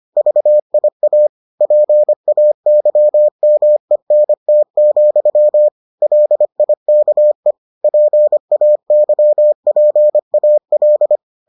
Morse code, heard as a keyed tone, about 25 wpm, 600 Hz. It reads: VIA PAYMENT, LIKE PAYPAL